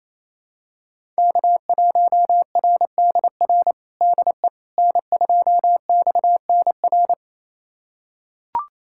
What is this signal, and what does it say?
Morse code, heard as a keyed tone, about 28 wpm, 700 Hz. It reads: K1RDR DE N2XNR
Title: USA Callsign DE Callsign - 5 Characters - 28wpm